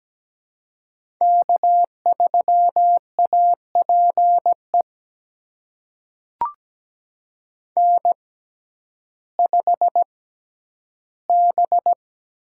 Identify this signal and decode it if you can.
Morse code, heard as a keyed tone, about 17 wpm, spaced as for 10 wpm, 700 Hz. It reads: K3APE N5B
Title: USA Call Signs - 5 Characters - 6x Character Spacing - 17wpm